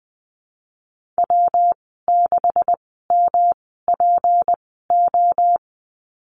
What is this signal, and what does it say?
Morse code, heard as a keyed tone, about 20 wpm, spaced as for 15 wpm, 700 Hz. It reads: W6MPO
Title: 179 Lesson 29g - Slash - Call Signs 20wpm